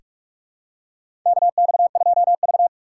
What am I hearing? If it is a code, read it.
Morse code, heard as a keyed tone, about 46 wpm, 700 Hz. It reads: KX2V
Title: USA Call Signs - 4 Characters - 4x Character Spacing - 45wpm